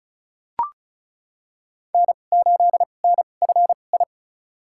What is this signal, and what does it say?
Morse code, heard as a keyed tone, about 36 wpm, spaced as for 25 wpm, 700 Hz. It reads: N8NFI
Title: USA Call Signs - 5 Characters - 2x Character Spacing - 35wpm